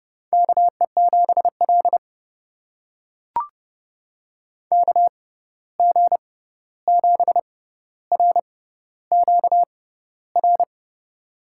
Morse code, heard as a keyed tone, about 31 wpm, spaced as for 14 wpm, 700 Hz. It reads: KE7L KG7RQR
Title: USA Call Signs - 6x Character Spacing - 30wpm